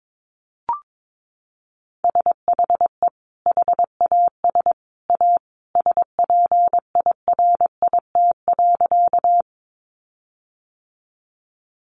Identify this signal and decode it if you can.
Morse code, heard as a keyed tone, about 22 wpm, 700 Hz. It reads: SHE HAS A SPIRIT.
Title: Sentences from Top 1400 Words 22wpm